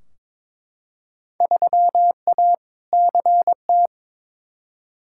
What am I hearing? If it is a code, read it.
Morse code, heard as a keyed tone, about 21 wpm, 700 Hz. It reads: 3A CT
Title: Field Day Exchanges - 22wpm